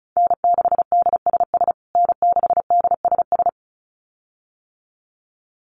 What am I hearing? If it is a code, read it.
Morse code, heard as a keyed tone, about 34 wpm, 700 Hz. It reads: N6DSS N6DSS
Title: USA Call Signs Repeated 2x 35wpm